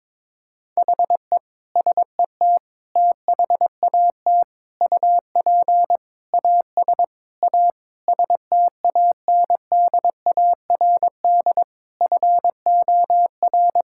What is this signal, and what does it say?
Morse code, heard as a keyed tone, about 22 wpm, 700 Hz. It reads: HE SET THAT UP AS A STANDARD FOR
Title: Sentences from Top 1000 Words - Review Part 6 - 22wpm